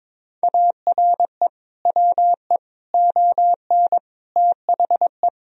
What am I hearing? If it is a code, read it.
Morse code, heard as a keyed tone, about 22 wpm, 700 Hz. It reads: ARE WE ON THE
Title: Sentences from Top 1000 Words - Review Part 3 - 22wpm